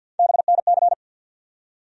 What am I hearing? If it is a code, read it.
Morse code, heard as a keyed tone, about 48 wpm, 700 Hz. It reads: BNC
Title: USA Call Sign Postfixes - 3 Letters 50wpm